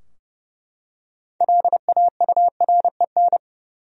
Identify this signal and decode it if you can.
Morse code, heard as a keyed tone, about 32 wpm, 700 Hz. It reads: LAUREN
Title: USA Common Names - 30wpm